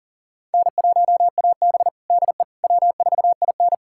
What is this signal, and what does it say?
Morse code, heard as a keyed tone, about 40 wpm, 700 Hz. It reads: N1AB DE W4IN
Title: USA Callsign DE Callsign - 4 Characters - 40wpm